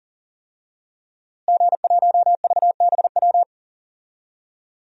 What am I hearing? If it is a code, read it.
Morse code, heard as a keyed tone, about 40 wpm, 700 Hz. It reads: G1VBW
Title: DXCC-100 Call Signs - 5 Characters - Repeated 2x - 40wpm